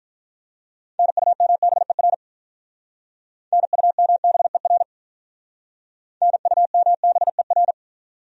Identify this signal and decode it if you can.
Morse code, heard as a keyed tone, about 44 wpm, 700 Hz. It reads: NUMBER NUMBER NUMBER